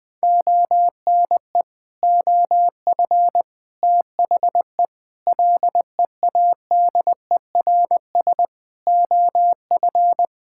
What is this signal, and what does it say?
Morse code, heard as a keyed tone, about 20 wpm, 700 Hz. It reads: ONE OF THE LEADERS OF
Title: Sets of 6 Words - 20wpm